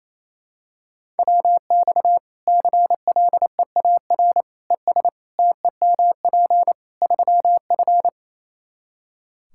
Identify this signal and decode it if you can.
Morse code, heard as a keyed tone, about 28 wpm, 700 Hz. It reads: WX CLEAR ES TEMP 3F
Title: Practice QSO - Exchange 2 - Info - 28wpm